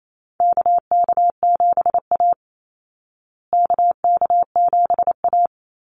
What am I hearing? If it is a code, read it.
Morse code, heard as a keyed tone, about 28 wpm, 700 Hz. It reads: KK7A KK7A
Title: Call Signs Repeated 2x - 28wpm